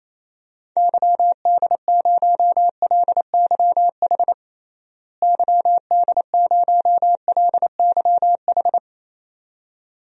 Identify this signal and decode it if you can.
Morse code, heard as a keyed tone, about 28 wpm, 700 Hz. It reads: YD0LYH YD0LYH
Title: DXCC-100 Call Signs - 6 Characters - Repeated 2x - 28wpm